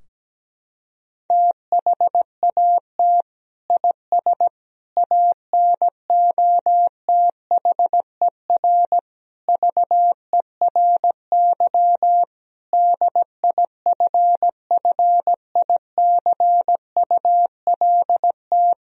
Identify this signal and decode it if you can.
Morse code, heard as a keyed tone, about 17 wpm, 700 Hz. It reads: THAT IS ANOTHER VERY DIFFICULT
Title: Sentences from Top 1000 Words - Review Part 2 - 17wpm